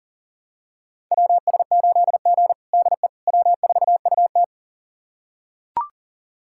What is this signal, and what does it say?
Morse code, heard as a keyed tone, about 37 wpm, 700 Hz. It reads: WS8G DE W4UT
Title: USA Callsign DE Callsign - 4 Characters - 40wpm